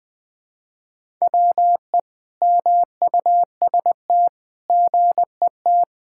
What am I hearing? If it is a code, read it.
Morse code, heard as a keyed tone, about 21 wpm, 700 Hz. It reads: WE MUST GET
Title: Sentences from Top 1000 Words - Review Part 6 - 20wpm